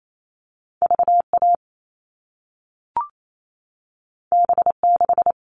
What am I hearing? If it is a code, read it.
Morse code, heard as a keyed tone, about 28 wpm, 700 Hz. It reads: VA B6
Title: All Cut Numbers - 2-Digit 28wpm